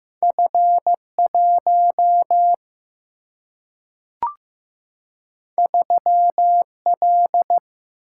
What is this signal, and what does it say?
Morse code, heard as a keyed tone, about 15 wpm, 700 Hz. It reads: F 1 3 L
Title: Single-Letter Single-Number 15wpm